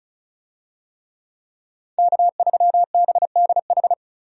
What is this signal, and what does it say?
Morse code, heard as a keyed tone, about 34 wpm, 700 Hz. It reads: K3BDH
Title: USA Call Signs - 5 Characters - 35wpm